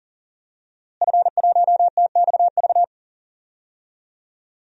Morse code, heard as a keyed tone, about 40 wpm, 700 Hz. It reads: F1TXV
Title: DXCC-100 Call Signs - 5 Characters - Repeated 2x - 40wpm